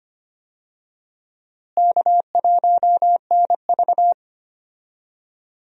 Morse code, heard as a keyed tone, about 25 wpm, 700 Hz. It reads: K1NV
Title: USA Call Signs - 4 Characters - Repeated 3x - 25wpm